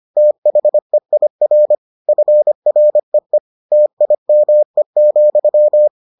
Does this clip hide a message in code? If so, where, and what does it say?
Yes: Morse code from 0.2 to 5.9 s, heard as a keyed tone, about 25 wpm, 600 Hz. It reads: THEIR FREE TIME,